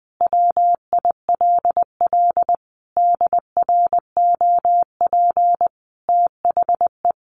Morse code, heard as a keyed tone, about 20 wpm, 700 Hz. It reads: WILL DROP THE
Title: Sentences from Top 1400 Words 20wpm